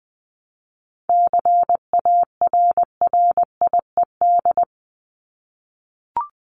Morse code, heard as a keyed tone, about 20 wpm, 700 Hz. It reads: CARRIED
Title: ED Suffix - Top 100 Words with Base Words 20wpm